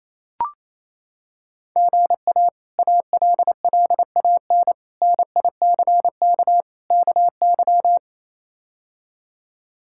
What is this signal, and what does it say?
Morse code, heard as a keyed tone, about 27 wpm, 700 Hz. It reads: GA ALLAN NICK KY
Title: K1USN SST QSO Practice - 28wpm